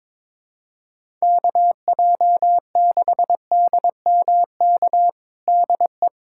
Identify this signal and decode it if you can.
Morse code, heard as a keyed tone, about 22 wpm, 700 Hz. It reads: KJ6DMK DE
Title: USA Callsign DE Callsign - 6 Characters - 22wpm